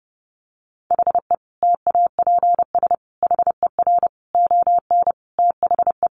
Morse code, heard as a keyed tone, about 30 wpm, 700 Hz. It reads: HE TAPS HER ON THE
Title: Additional Easy Phrases 30wpm